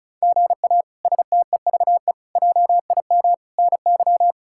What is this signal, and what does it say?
Morse code, heard as a keyed tone, about 36 wpm, 700 Hz. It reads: GA STEVE JIM NY
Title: K1USN SST QSO Practice - 35wpm